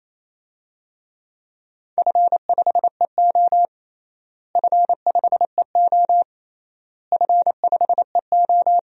Morse code, heard as a keyed tone, about 28 wpm, 700 Hz. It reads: F5EO F5EO F5EO
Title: DXCC-100 Call Signs - 4 Characters - Repeated 3x - 28wpm